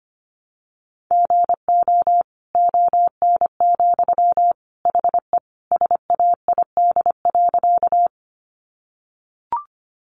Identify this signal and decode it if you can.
Morse code, heard as a keyed tone, about 25 wpm, 700 Hz. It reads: GO ON, HE SAID.
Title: Sentences from Top 100 Words 25wpm